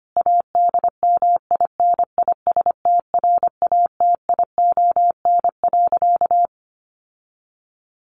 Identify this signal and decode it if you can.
Morse code, heard as a keyed tone, about 25 wpm, 700 Hz. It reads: ADMINISTRATION.
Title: Sentences from Top 1300 Words 25wpm